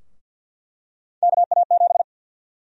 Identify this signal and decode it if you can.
Morse code, heard as a keyed tone, about 49 wpm, 700 Hz. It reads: KA7
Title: USA Call Sign Prefixes - 50wpm